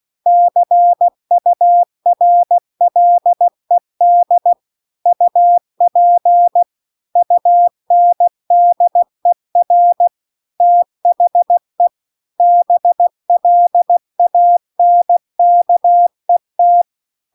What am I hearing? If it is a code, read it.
Morse code, heard as a keyed tone, about 16 wpm, 700 Hz. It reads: CURLED UP UNDER THE BLANKET